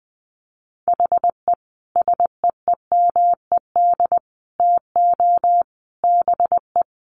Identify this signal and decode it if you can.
Morse code, heard as a keyed tone, about 20 wpm, 700 Hz. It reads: HE SEEMED TO BE
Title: A Sound Of Thunder 20wpm